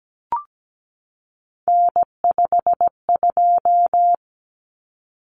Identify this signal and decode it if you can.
Morse code, heard as a keyed tone, about 17 wpm, 700 Hz. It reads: N52
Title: Common Cut Numbers - 3-Digit 17wpm